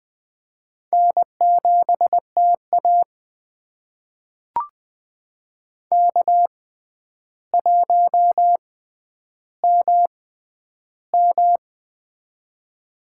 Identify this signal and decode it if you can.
Morse code, heard as a keyed tone, about 20 wpm, spaced as for 10 wpm, 700 Hz. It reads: N7TA K1MM
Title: USA Call Signs - 4 Characters - 6x Character Spacing - 20wpm